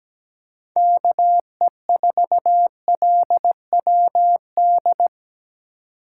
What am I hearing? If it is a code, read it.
Morse code, heard as a keyed tone, about 17 wpm, 700 Hz. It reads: KE4LWD
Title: USA Call Signs - 8x Character Spacing - 17wpm